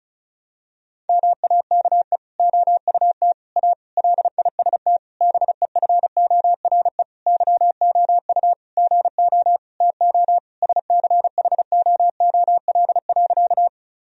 Morse code, heard as a keyed tone, about 34 wpm, 700 Hz. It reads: MAKE OUT A LIST BEFORE YOU GO TO SCHOOL.
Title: Sentences from Top 1000 Words - Review Part 2 - 35wpm